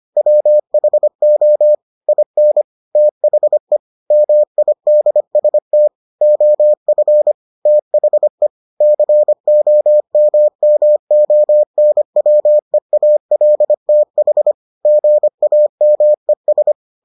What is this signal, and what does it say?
Morse code, heard as a keyed tone, about 25 wpm, 600 Hz. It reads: WHO IN THE MIDST OF THE COMMONWEALTH GAMES